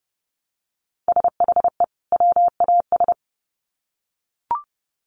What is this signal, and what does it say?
Morse code, heard as a keyed tone, about 30 wpm, 700 Hz. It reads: SHE WAS
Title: Sets of 2 Words 30wpm